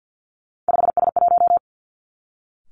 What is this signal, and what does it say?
Morse code, heard as a keyed tone, about 50 wpm, 700 Hz. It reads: 5S1
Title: Three Challenging Letter-Number Combinations 50wpm